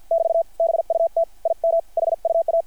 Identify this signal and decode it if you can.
Morse code, heard as a keyed tone, about 50 wpm, 660 Hz. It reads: <BT> BUT IM HUA
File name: archived / CRSnet